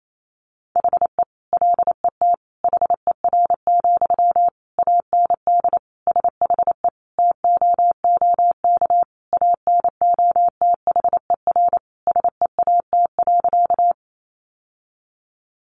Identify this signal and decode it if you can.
Morse code, heard as a keyed tone, about 28 wpm, 700 Hz. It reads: HE LET HER, AND SHE TOOK ANOTHER SEAT.
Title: Sentences from Top 1500 Words 28wpm